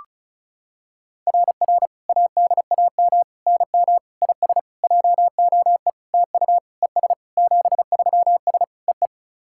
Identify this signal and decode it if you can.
Morse code, heard as a keyed tone, about 36 wpm, 700 Hz. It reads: RR ADAM NM IS JOE TU ES 73S EE
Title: SOTA QSO Practice 35wpm